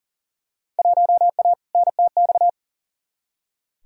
Morse code, heard as a keyed tone, about 40 wpm, 700 Hz. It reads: 1A NTX
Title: Field Day Exchanges - 40wpm